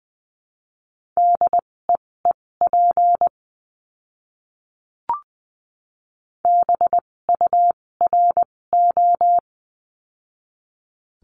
Morse code, heard as a keyed tone, about 20 wpm, spaced as for 16 wpm, 700 Hz. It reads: DEEP BURO
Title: 256 Lesson 42a - Words 12-20wpm